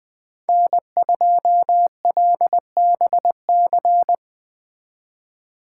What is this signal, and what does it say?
Morse code, heard as a keyed tone, about 20 wpm, 700 Hz. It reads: N2LBC
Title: Speed Racing - USA 1x3 Call Signs 17wpm